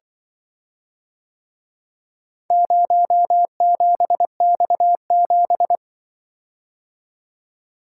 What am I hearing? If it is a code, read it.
Morse code, heard as a keyed tone, about 24 wpm, 700 Hz. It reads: 07X7
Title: Speed-Racing - Four Letter-Number Combinations - 20wpm